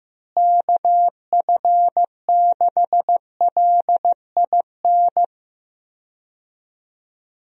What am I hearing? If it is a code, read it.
Morse code, heard as a keyed tone, about 15 wpm, 700 Hz. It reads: KF6LIN
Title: USA Call Signs - 6 Characters - Repeated 2x - 15wpm